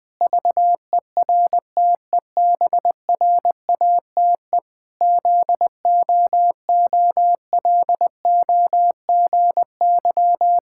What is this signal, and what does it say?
Morse code, heard as a keyed tone, about 21 wpm, 700 Hz. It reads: VERTEBRATE ZOOLOGY
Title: Pangram Challenge v2 - 20wpm